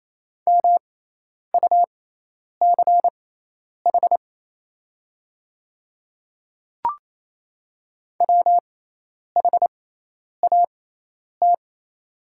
Morse code, heard as a keyed tone, about 27 wpm, spaced as for 10 wpm, 700 Hz. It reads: MUCH WHAT
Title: 6x Character Spacing - Top 300 Words - 28wpm